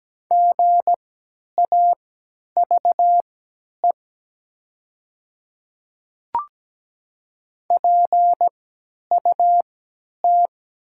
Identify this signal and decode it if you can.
Morse code, heard as a keyed tone, about 17 wpm, spaced as for 10 wpm, 700 Hz. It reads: GAVE PUT
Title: 3x Character Spacing - Top 500 Words - 17wpm